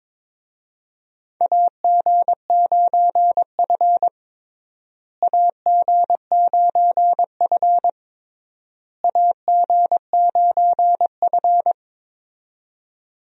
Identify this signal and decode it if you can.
Morse code, heard as a keyed tone, about 21 wpm, 700 Hz. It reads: AG9F AG9F AG9F
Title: USA Call Signs - 4 Characters - Repeated 3x - 22wpm